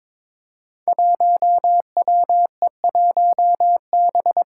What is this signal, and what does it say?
Morse code, heard as a keyed tone, about 22 wpm, 700 Hz. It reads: 1WE1B
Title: Five Challenging Letter-Number Combinations 22wpm